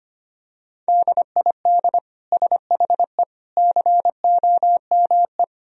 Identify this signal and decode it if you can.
Morse code, heard as a keyed tone, about 25 wpm, 700 Hz. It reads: DID SHE COME
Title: Sentences from Top 1000 Words - Review Part 2 - 25wpm